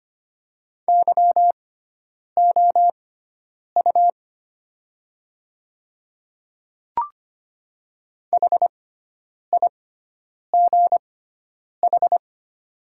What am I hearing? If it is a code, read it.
Morse code, heard as a keyed tone, about 25 wpm, spaced as for 9 wpm, 700 Hz. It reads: YOU HIGH
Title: 6x Character Spacing - Top 300 Words - 25wpm